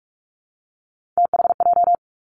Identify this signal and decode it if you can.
Morse code, heard as a keyed tone, about 46 wpm, 700 Hz. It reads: THJ
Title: Three Challenging Letter-Number Combinations 45wpm